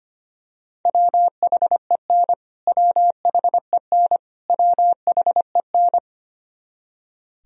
Morse code, heard as a keyed tone, about 25 wpm, 700 Hz. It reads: WHEN WHEN WHEN
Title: Top 100 Words Repeated 25wpm